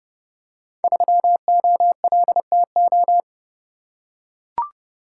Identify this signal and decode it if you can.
Morse code, heard as a keyed tone, about 30 wpm, 700 Hz. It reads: 3OLTO
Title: Code Groups - 5 Letter:Numbers - 30wpm